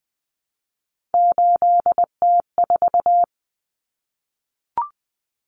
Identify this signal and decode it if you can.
Morse code, heard as a keyed tone, about 19 wpm, 700 Hz. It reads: 8T4
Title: Common Cut Numbers - 3-Digit 20wpm